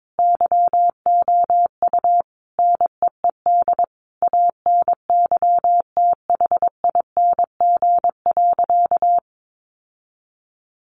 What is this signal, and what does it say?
Morse code, heard as a keyed tone, about 22 wpm, 700 Hz. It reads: YOU NEED ANYTHING.
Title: Sentences from Top 1000 Words 22wpm